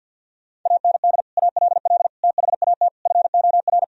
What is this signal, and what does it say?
Morse code, heard as a keyed tone, about 50 wpm, 700 Hz. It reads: AND ALL THAT FOR
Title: Sentences from Top 1000 Words - Review Part 4 - 50wpm